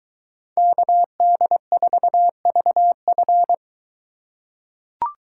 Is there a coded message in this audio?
Yes: Morse code from 0.6 to 3.5 s, heard as a keyed tone, about 23 wpm, 700 Hz. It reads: KD4VF